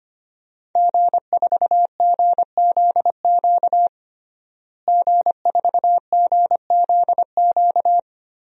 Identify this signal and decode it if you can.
Morse code, heard as a keyed tone, about 25 wpm, 700 Hz. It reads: G4GZQ G4GZQ
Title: DXCC-100 Call Signs - 5 Characters - Repeated 3x - 25wpm